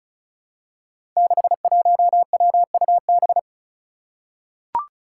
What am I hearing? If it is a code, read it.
Morse code, heard as a keyed tone, about 36 wpm, 700 Hz. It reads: 61WUB
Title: Five Challenging Letter-Number Combinations 35wpm